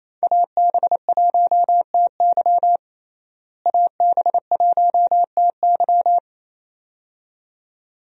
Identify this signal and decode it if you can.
Morse code, heard as a keyed tone, about 28 wpm, 700 Hz. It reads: AB1TY AB1TY
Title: USA Call Signs - 5 Characters - Repeated 2x - 28wpm